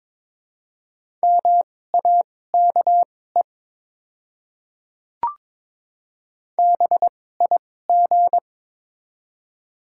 Morse code, heard as a keyed tone, about 22 wpm, spaced as for 16 wpm, 690 Hz. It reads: MAKE BIG
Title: 2x Character Spacing - Top 500 Words - 22wpm